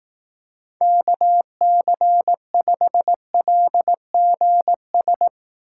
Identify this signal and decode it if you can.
Morse code, heard as a keyed tone, about 18 wpm, 700 Hz. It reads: KC5LGS
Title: Speed Racing - USA 2x3 Call Signs 15wpm